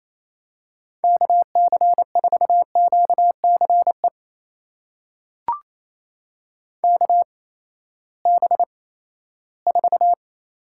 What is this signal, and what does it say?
Morse code, heard as a keyed tone, about 28 wpm, spaced as for 16 wpm, 700 Hz. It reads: KC4QCE KB4
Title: USA Call Signs - 6 Characters - 8x Character Spacing - 28wpm